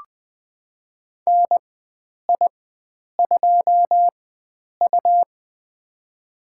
Morse code, heard as a keyed tone, about 20 wpm, spaced as for 9 wpm, 700 Hz. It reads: NI2U
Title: USA Call Signs - 4 Characters - 4x Character Spacing - 20wpm